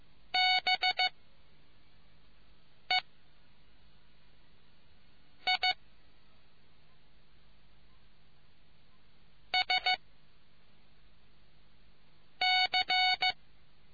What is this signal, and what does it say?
Morse code, heard as a keyed tone, about 15 wpm, 740 Hz. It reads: B E I S C